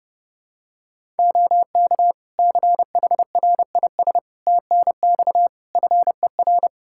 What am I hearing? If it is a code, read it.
Morse code, heard as a keyed tone, about 30 wpm, 700 Hz. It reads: OK CHRIS TNX FER
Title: Practice QSO - Exchange 4 - Ending - 30wpm